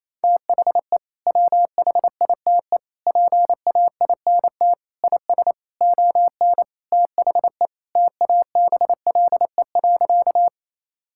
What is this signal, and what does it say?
Morse code, heard as a keyed tone, about 28 wpm, 700 Hz. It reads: THE WHITE PAINT IS ON THE TABLE.
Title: Sentences from Top 1000 Words - Review Part 6 - 28wpm